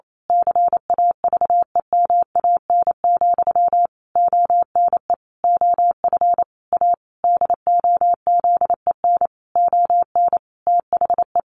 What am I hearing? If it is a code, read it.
Morse code, heard as a keyed tone, about 28 wpm, 700 Hz. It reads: CAVEMAN, ONE OF A DOZEN ON THE
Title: A Sound Of Thunder 28wpm